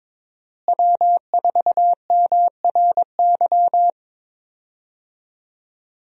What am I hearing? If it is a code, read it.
Morse code, heard as a keyed tone, about 21 wpm, 700 Hz. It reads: W4MRY
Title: USA Call Signs - 5 Characters - Repeated 2x - 22wpm